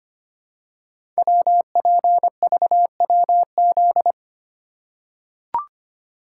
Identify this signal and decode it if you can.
Morse code, heard as a keyed tone, about 25 wpm, 700 Hz. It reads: WPVWZ
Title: Code Groups - 5 Letters - 25wpm